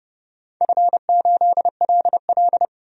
Morse code, heard as a keyed tone, about 30 wpm, 700 Hz. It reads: F8LL
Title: DXCC-100 Call Signs - 4 Characters - 30wpm